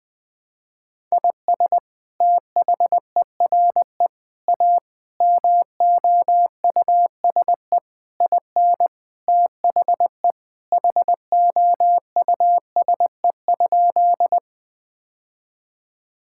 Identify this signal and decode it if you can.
Morse code, heard as a keyed tone, about 21 wpm, 700 Hz. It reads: IS THERE A MOUSE IN THE HOUSE?
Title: Sentences from Top 1000 Words - Review Part 5 - 20wpm